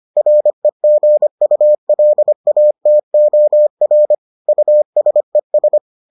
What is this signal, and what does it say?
Morse code, heard as a keyed tone, about 25 wpm, 600 Hz. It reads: REGULATOR USES